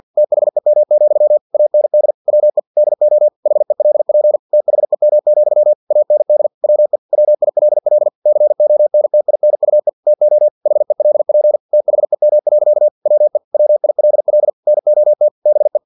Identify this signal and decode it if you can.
Morse code, heard as a keyed tone, about 50 wpm, 600 Hz. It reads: THEM, AND WE DO HELP THEM, AND WE WILL CONTINUE TO HELP THEM. WE WILL NOT BE
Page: Never 49wpm